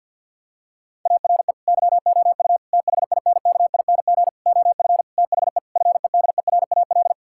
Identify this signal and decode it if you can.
Morse code, heard as a keyed tone, about 50 wpm, 700 Hz. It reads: ARE YOU THINKING OF THE FEDERAL
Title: Sentences from Top 1000 Words - Review Part 6 - 50wpm